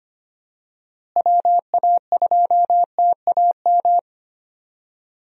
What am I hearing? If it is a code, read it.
Morse code, heard as a keyed tone, about 25 wpm, 700 Hz. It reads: WA2TAM
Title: USA Call Signs - 6 Characters - Repeated 3x - 25wpm